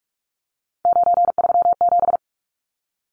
Morse code, heard as a keyed tone, about 44 wpm, 700 Hz. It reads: 937
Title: Numbers - 3-Digit 45wpm